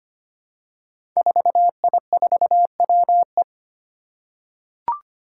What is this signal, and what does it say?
Morse code, heard as a keyed tone, about 25 wpm, 700 Hz. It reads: 4I4WE
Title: Five Challenging Letter-Number Combinations 25wpm